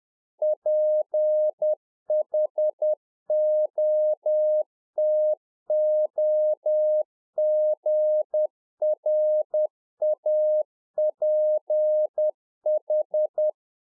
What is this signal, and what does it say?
Morse code, heard as a keyed tone, about 10 wpm, 610 Hz. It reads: PHOTOGRAPH
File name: 2011-04-mp3 / RICK JJ0MPI 10WPM